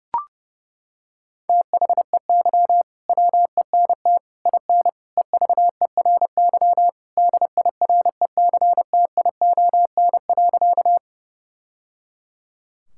Morse code, heard as a keyed tone, about 32 wpm, 700 Hz. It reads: THEY WENT IN EVERY DIRECTION.
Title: Sentences from Top 1000 Words - Review Part 5 - 30wpm